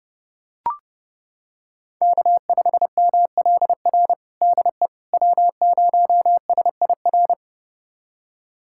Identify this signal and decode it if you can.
Morse code, heard as a keyed tone, about 30 wpm, 700 Hz. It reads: K5MLR DE W0SIR
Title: USA Callsign DE Callsign - 5 Characters - 30wpm